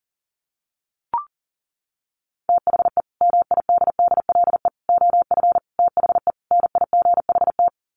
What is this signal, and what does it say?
Morse code, heard as a keyed tone, about 40 wpm, 700 Hz. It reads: THE MIDDLE OF THE NIGHT
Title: Sets of 5 Words 40wpm